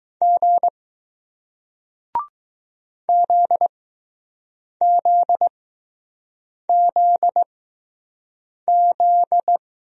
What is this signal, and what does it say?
Morse code, heard as a keyed tone, about 21 wpm, spaced as for 6 wpm, 700 Hz. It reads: G ZZZZ